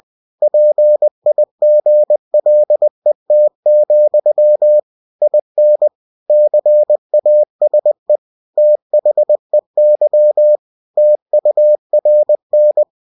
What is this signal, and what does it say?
Morse code, heard as a keyed tone, about 20 wpm, 600 Hz. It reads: PIGLET, IN CASE THEY TURN
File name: cw-books / winnie-the-pooh / winnie-20wpm-page-0022